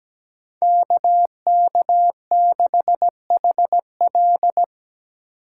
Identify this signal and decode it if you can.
Morse code, heard as a keyed tone, about 20 wpm, 700 Hz. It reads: KK6HL
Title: Speed Racing - USA 2x2 Call Signs 17wpm